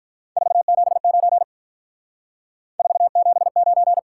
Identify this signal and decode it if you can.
Morse code, heard as a keyed tone, about 46 wpm, 700 Hz. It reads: 479 479